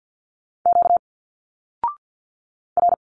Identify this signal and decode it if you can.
Morse code, heard as a keyed tone, about 48 wpm, 700 Hz. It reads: Q R
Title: Single Letters 50wpm